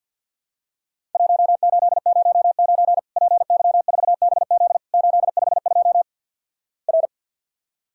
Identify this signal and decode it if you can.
Morse code, heard as a keyed tone, about 50 wpm, 700 Hz. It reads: 1809 PY4BZ 852 R
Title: Worked All Europe DX Contest - QTC Exchange - 50wpm